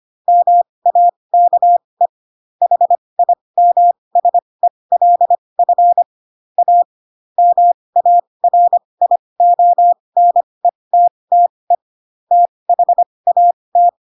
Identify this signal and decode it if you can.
Morse code, heard as a keyed tone, about 25 wpm, spaced as for 20 wpm, 700 Hz. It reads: MAKE HIMSELF A MARIONETTE THAT
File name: Pinocchio / Pinocchio ch2 15 25 ik1hge